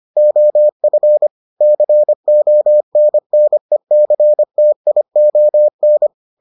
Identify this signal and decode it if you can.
Morse code, heard as a keyed tone, about 25 wpm, 600 Hz. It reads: OF CONNECTION